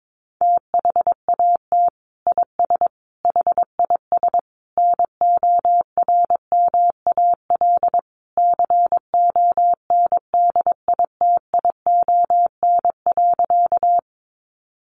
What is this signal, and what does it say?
Morse code, heard as a keyed tone, about 22 wpm, 700 Hz. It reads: THAT IS HIS NORMAL CONDITION.
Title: Sentences from Top 1200 Words 22wpm